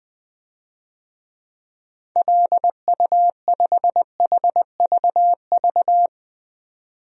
Morse code, heard as a keyed tone, about 21 wpm, 700 Hz. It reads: LU5HVV
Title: DXCC-100 Call Signs - 6 Characters - 20wpm